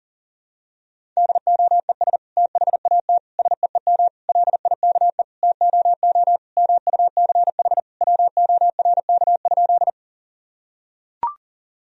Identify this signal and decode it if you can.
Morse code, heard as a keyed tone, about 40 wpm, 700 Hz. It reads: DOES THAT SEEM LIKE TOO MUCH WORK?
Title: Sentences from Top 1000 Words - Review Part 1 - 40wpm